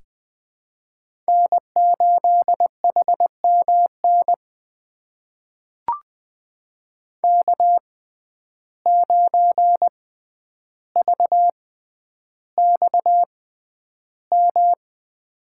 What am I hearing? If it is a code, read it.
Morse code, heard as a keyed tone, about 20 wpm, spaced as for 10 wpm, 700 Hz. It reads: N8HMN K9VXM
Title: USA Call Signs - 5 Characters - 6x Character Spacing - 20wpm